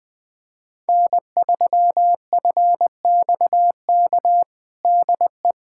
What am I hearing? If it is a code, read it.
Morse code, heard as a keyed tone, about 20 wpm, 700 Hz. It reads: N3FXK DE
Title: USA Callsign DE Callsign - 5 Characters - 20wpm